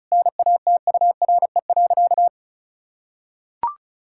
Morse code, heard as a keyed tone, about 36 wpm, 700 Hz. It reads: NATURE.
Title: Sentences from Top 1000 Words - Review Part 6 - 35wpm